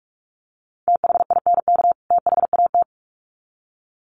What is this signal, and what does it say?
Morse code, heard as a keyed tone, about 44 wpm, 700 Hz. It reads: THINK THAT
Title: Sets of 2 Words 45wpm